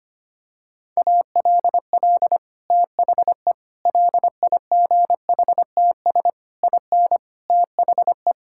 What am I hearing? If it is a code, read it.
Morse code, heard as a keyed tone, about 25 wpm, 700 Hz. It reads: ALL THE LIGHTS IN THE
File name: practice / 25wpm / Sentences with 2-Word Phrases - 25wpm